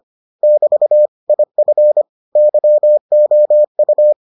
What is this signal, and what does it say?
Morse code, heard as a keyed tone, about 25 wpm, 600 Hz. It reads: <BT> IF YOU